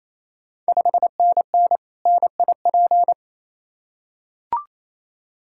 Morse code, heard as a keyed tone, about 28 wpm, 700 Hz. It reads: 5NN NIP
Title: Ontario QSO Party Exchange 28wpm